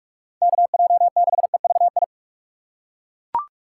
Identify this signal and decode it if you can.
Morse code, heard as a keyed tone, about 41 wpm, 700 Hz. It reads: KJ6EVI
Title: USA Call Signs - 6 Characters - 6x Character Spacing - 45wpm